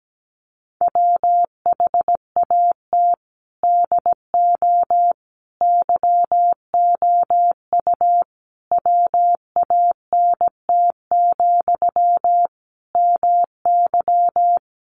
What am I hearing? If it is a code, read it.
Morse code, heard as a keyed tone, about 17 wpm, 700 Hz. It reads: WHAT DO YOU WANT, MY
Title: Sentences from Top 1000 Words 17wpm